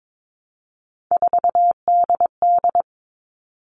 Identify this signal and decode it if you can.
Morse code, heard as a keyed tone, about 22 wpm, 700 Hz. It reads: 4DD
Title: All Cut Numbers - 3-Digit 22wpm